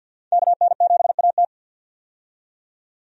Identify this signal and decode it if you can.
Morse code, heard as a keyed tone, about 50 wpm, 700 Hz. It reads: KN7AT
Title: USA Call Signs - 5 Characters - 50wpm